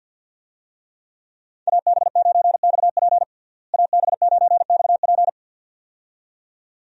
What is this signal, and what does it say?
Morse code, heard as a keyed tone, about 50 wpm, 700 Hz. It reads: AB9XP AB9XP
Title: USA Call Signs - 5 Characters - Repeated 2x - 50wpm